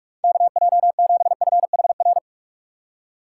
Morse code, heard as a keyed tone, about 46 wpm, 700 Hz. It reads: KJ7FHR
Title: USA Call Signs - 6 Characters - 3x Character Spacing - 45wpm